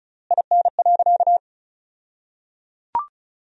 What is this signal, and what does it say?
Morse code, heard as a keyed tone, about 34 wpm, 700 Hz. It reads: IN.
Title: 2x Word-Spacing - Sentences from Top 300 Words - 35wpm